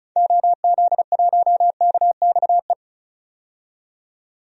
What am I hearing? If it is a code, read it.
Morse code, heard as a keyed tone, about 36 wpm, 700 Hz. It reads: OZ1KXE
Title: DXCC-100 Call Signs - 6 Characters - 35wpm